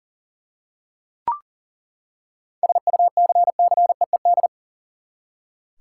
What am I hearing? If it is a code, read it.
Morse code, heard as a keyed tone, about 39 wpm, 700 Hz. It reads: SUCCEED
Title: Seven Letter Words - 40wpm